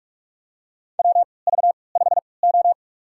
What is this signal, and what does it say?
Morse code, heard as a keyed tone, about 45 wpm, spaced as for 25 wpm, 700 Hz. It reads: WV5O
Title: USA Call Signs - 4 Characters - 3x Character Spacing - 45wpm